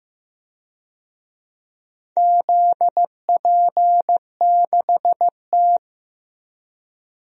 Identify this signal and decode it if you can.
Morse code, heard as a keyed tone, about 15 wpm, 700 Hz. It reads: ZP6T
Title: DXCC-100 Call Signs - 4 Characters - Repeated 2x - 15wpm